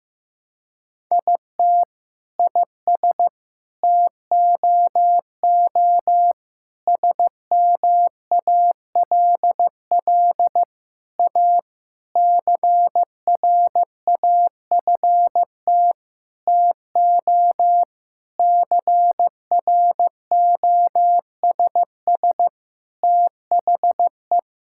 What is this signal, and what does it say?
Morse code, heard as a keyed tone, about 15 wpm, 700 Hz. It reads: IT IS TOO SMALL A CRAFT TO CROSS THE
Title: Sentences from Top 1000 Words - Review Part 6 - 15wpm